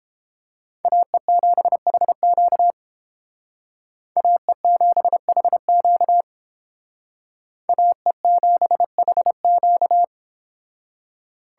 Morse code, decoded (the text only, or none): AE7HQ AE7HQ AE7HQ